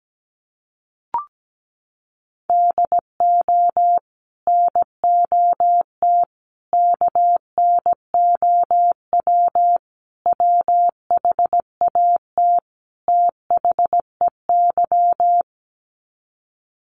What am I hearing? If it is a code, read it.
Morse code, heard as a keyed tone, about 17 wpm, 700 Hz. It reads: DO NOT KNOW WHAT THEY
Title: Sets of 5 Words 17wpm